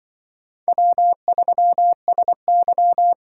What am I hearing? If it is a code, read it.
Morse code, heard as a keyed tone, about 25 wpm, 700 Hz. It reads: W3SY
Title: Speed Racing - USA 1x2 Call Signs 20wpm